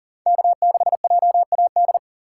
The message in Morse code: K6JAD

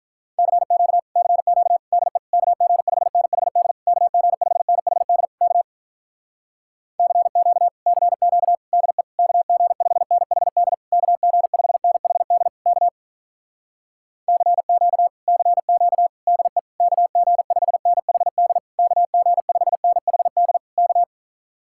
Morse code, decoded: CQ CQ DE KG5NHD KG5NHD K CQ CQ DE KG5NHD KG5NHD K CQ CQ DE KG5NHD KG5NHD K

